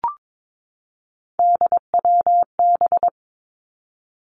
Morse code, heard as a keyed tone, about 22 wpm, 700 Hz. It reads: DWB
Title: Three Challenging Letter-Number Combinations 22wpm